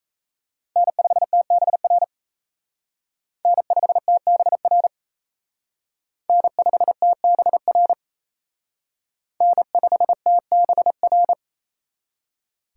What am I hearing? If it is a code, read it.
Morse code, heard as a keyed tone, about 34 wpm, 700 Hz. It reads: N5TBR N5TBR N5TBR N5TBR
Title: Speed Racing - USA 1x3 Call Signs 28wpm